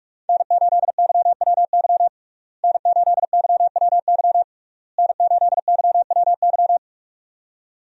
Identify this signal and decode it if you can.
Morse code, heard as a keyed tone, about 41 wpm, 700 Hz. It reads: N8YWY N8YWY N8YWY
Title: USA Call Signs - 5 Characters - Repeated 3x - 45wpm